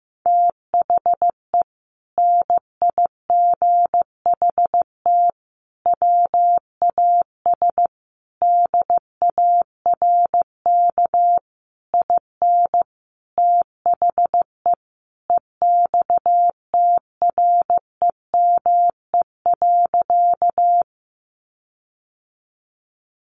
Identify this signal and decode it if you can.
Morse code, heard as a keyed tone, about 15 wpm, 700 Hz. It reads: THE NIGHT WAS DARK IN THE EXTREME.
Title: Sentences from Top 1300 Words 15wpm